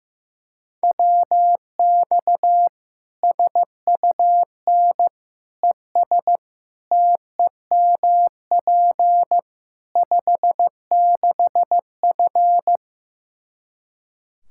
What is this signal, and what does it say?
Morse code, heard as a keyed tone, about 15 wpm, 700 Hz. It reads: WX SUN ES TEMP 56F
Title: Practice QSO - Exchange 2 - Info - 15wpm